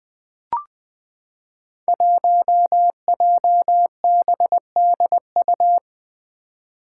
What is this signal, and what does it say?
Morse code, heard as a keyed tone, about 20 wpm, 700 Hz. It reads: 1JBDU
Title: Five Challenging Letter-Number Combinations 20wpm